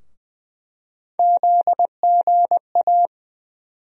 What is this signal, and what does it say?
Morse code, heard as a keyed tone, about 20 wpm, 700 Hz. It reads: ZGA
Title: USA Call Sign Postfixes - 3 Letters 20wpm